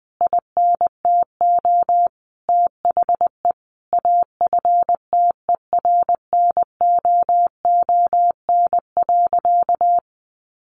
Morse code, heard as a keyed tone, about 19 wpm, 700 Hz. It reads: INTO THE AFTERNOON.
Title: Sentences from Top 1500 Words 20wpm